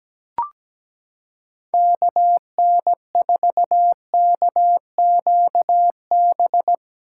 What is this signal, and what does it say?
Morse code, heard as a keyed tone, about 17 wpm, 700 Hz. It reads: KN4KQB
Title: USA Callsign DE Callsign 17wpm